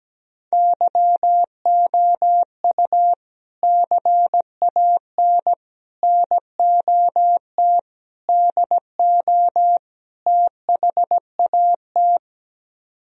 Morse code, heard as a keyed tone, about 17 wpm, 700 Hz. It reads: YOU CAN NOT DO THAT
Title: Speed-Racing - Sets of 5 Words 17wpm